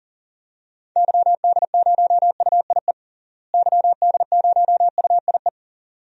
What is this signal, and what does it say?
Morse code, heard as a keyed tone, about 40 wpm, 700 Hz. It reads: YD0UIE YD0UIE
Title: DXCC-100 Call Signs - 6 Characters - Repeated 2x - 40wpm